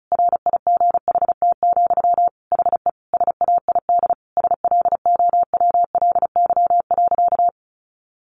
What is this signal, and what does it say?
Morse code, heard as a keyed tone, about 34 wpm, 700 Hz. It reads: RIGHT, HE SAID SLOWLY.
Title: Sentences from Top 1100 Words 35wpm